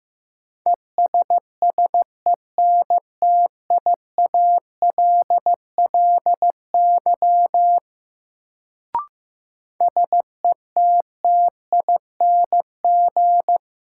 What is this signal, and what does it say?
Morse code, heard as a keyed tone, about 15 wpm, 700 Hz. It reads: ESSENTIALLY SETTING
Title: Sentences from Top 1300 Words - New Word Drill - 15wpm